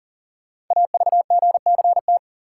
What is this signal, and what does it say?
Morse code, heard as a keyed tone, about 42 wpm, 690 Hz. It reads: AVGCT
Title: Code Groups - 5 Letter:Numbers - 40wpm